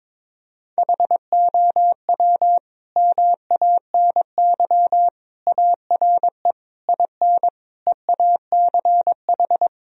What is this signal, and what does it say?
Morse code, heard as a keyed tone, about 22 wpm, 700 Hz. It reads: HOW MANY ARE IN EACH
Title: Sentences from Top 1000 Words - Review Part 2 - 22wpm